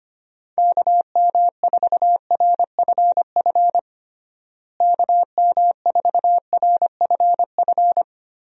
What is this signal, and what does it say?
Morse code, heard as a keyed tone, about 25 wpm, 700 Hz. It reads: KM4RFF KM4RFF
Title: USA Call Signs - 6 Characters - Repeated 3x - 25wpm